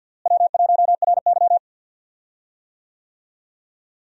Morse code, heard as a keyed tone, about 52 wpm, 700 Hz. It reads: W1RY
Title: USA Call Signs - 4 Characters - Repeated 3x - 50wpm